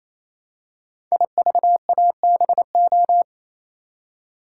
Morse code, heard as a keyed tone, about 29 wpm, 700 Hz. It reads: IVABO
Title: Code Groups - 5 Letters - 28wpm